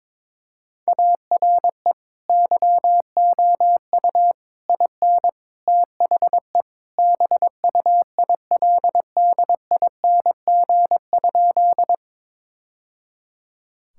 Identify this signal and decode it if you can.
Morse code, heard as a keyed tone, about 22 wpm, 700 Hz. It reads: ARE YOU IN THE BUILDING?
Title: Sentences from Top 1000 Words - Review Part 5 - 22wpm